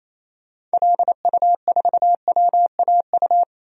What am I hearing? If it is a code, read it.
Morse code, heard as a keyed tone, about 28 wpm, 700 Hz. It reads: LU4WAU
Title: DXCC-100 Call Signs - 6 Characters - Repeated 3x - 28wpm